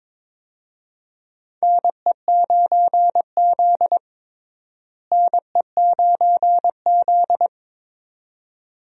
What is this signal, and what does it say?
Morse code, heard as a keyed tone, about 21 wpm, 700 Hz. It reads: NE9Z NE9Z
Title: USA Call Signs - 4 Characters - Repeated 2x - 22wpm